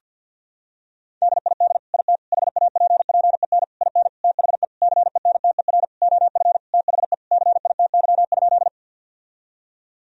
Morse code, heard as a keyed tone, about 53 wpm, 700 Hz. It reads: DID IT HAPPEN IN THE CENTER OF THE CITY?
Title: Sentences from Top 1000 Words - Review Part 2 - 50wpm